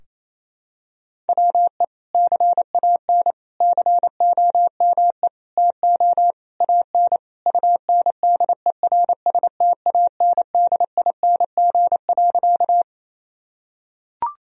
Morse code, heard as a keyed tone, about 28 wpm, 700 Hz. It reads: WE CAN COME TO AN UNDERSTANDING.
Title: Sentences from Top 1000 Words - Review Part 3 - 28wpm